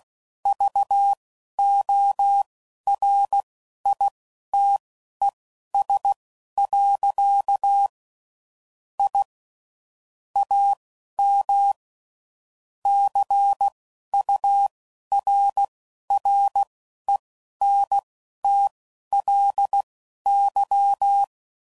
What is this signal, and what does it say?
Morse code, heard as a keyed tone, about 16 wpm, spaced as for 11 wpm, 800 Hz. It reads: VORITES. I AM CURRENTLY
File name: CW Code Groups @15WPM/QSOs/WD0FBT Spaced